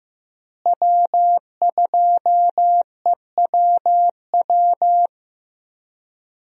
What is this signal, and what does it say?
Morse code, heard as a keyed tone, about 15 wpm, 700 Hz. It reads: W2EWW
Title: USA Call Signs - 5 Characters - Repeated 2x - 15wpm